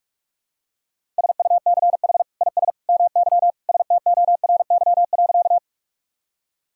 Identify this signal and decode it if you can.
Morse code, heard as a keyed tone, about 46 wpm, 700 Hz. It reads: SUCH IS MY STORY.
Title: Sentences from Top 1000 Words - Review Part 1 - 45wpm